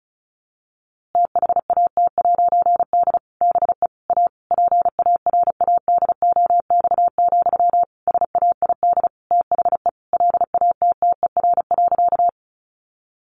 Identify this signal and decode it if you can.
Morse code, heard as a keyed tone, about 34 wpm, 700 Hz. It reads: THAT'D BE A PARADOX, SAID THE LATTER.
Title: A Sound Of Thunder 35wpm